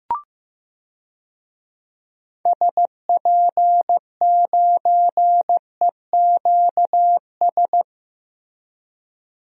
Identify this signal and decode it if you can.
Morse code, heard as a keyed tone, about 15 wpm, 700 Hz. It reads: SP9EQS
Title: DXCC-100 Call Signs - 6 Characters - Repeated 3x - 15wpm